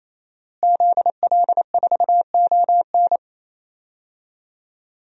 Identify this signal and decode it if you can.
Morse code, heard as a keyed tone, about 27 wpm, 700 Hz. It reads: ZL4ON
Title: DXCC-100 Call Signs - 5 Characters - Repeated 3x - 28wpm